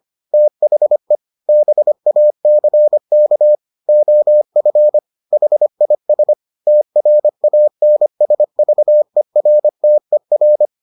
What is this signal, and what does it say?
Morse code, heard as a keyed tone, about 25 wpm, 600 Hz. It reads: THE BACK OF HIS TRANSVERTER